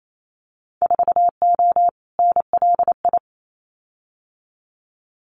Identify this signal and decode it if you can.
Morse code, heard as a keyed tone, about 28 wpm, 700 Hz. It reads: 4O NLI
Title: Winter Field Day Exchanges Encore 28wpm